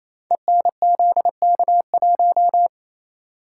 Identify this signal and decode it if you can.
Morse code, heard as a keyed tone, about 29 wpm, 700 Hz. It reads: ENZK1
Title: Code Groups - 5 Letter:Numbers - 28wpm